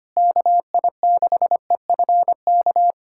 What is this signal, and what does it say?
Morse code, heard as a keyed tone, about 25 wpm, 700 Hz. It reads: KI6EFK
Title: USA Callsign DE Callsign - 6 Characters - 25wpm